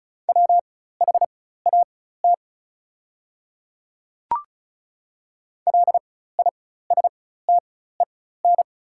Morse code, heard as a keyed tone, about 36 wpm, spaced as for 16 wpm, 690 Hz. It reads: WHAT LISTEN
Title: 4x Character Spacing - Top 500 Words - 35wpm